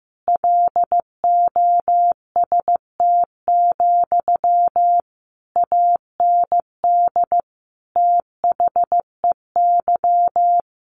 Morse code, heard as a keyed tone, about 15 wpm, 700 Hz. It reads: LOST, AND THEY
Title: Sentences from Top 1500 Words 15wpm